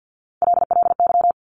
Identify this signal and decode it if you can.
Morse code, heard as a keyed tone, about 50 wpm, 700 Hz. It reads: LLY
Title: Top Three Letter Combinations in Words 50wpm